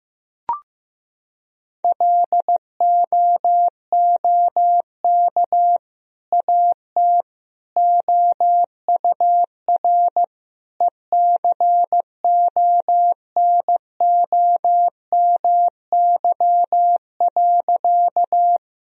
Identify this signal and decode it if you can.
Morse code, heard as a keyed tone, about 16 wpm, 700 Hz. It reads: LOOK AT OUR ECONOMY.
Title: Sentences from Top 1000 Words - Review Part 3 - 15wpm